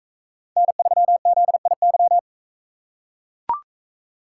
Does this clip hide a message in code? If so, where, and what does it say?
Yes: Morse code from 0.6 to 2.2 s, heard as a keyed tone, about 40 wpm, 700 Hz. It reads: N3ZIY